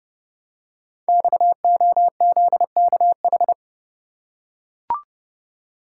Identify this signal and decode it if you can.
Morse code, heard as a keyed tone, about 28 wpm, 700 Hz. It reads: XOZKH
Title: Code Groups - 5 Letter:Numbers - 30wpm